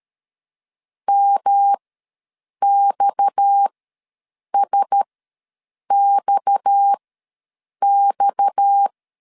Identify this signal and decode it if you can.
Morse code, heard as a keyed tone, about 13 wpm, spaced as for 7 wpm, 800 Hz. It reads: MXSXX
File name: LEZIONI / 6 PKX0 / 6 PKX ws 12wpm 3106